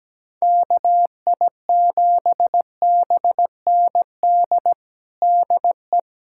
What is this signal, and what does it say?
Morse code, heard as a keyed tone, about 17 wpm, 700 Hz. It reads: KI7BND DE
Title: USA Callsign DE Callsign - 6 Characters - 17wpm